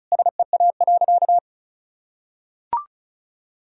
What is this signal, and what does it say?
Morse code, heard as a keyed tone, about 34 wpm, 700 Hz. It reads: SEA.
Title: Sentences from Top 1000 Words - Review Part 2 - 35wpm